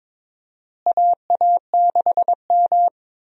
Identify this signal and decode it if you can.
Morse code, heard as a keyed tone, about 22 wpm, 700 Hz. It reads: AA6M
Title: USA Call Signs - 4 Characters - Repeated 3x - 22wpm